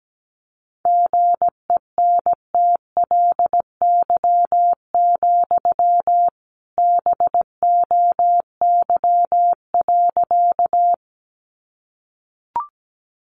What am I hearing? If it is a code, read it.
Morse code, heard as a keyed tone, about 17 wpm, 700 Hz. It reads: GENTLY, BOY.
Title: Sentences from Top 1500 Words 17wpm